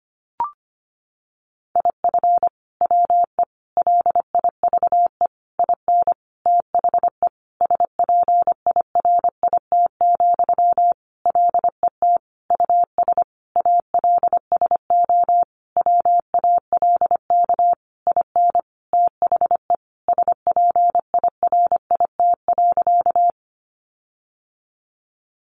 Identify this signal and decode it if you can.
Morse code, heard as a keyed tone, about 25 wpm, 700 Hz. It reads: IF WE LIVE IN THE SPIRIT, LET US ALSO WALK IN THE SPIRIT.
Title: Sentences from Top 1400 Words 25wpm